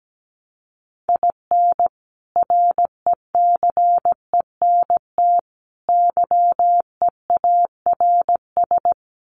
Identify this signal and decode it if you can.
Morse code, heard as a keyed tone, about 17 wpm, 700 Hz. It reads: IN RECENT YEARS
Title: Sets of 3 Words 17wpm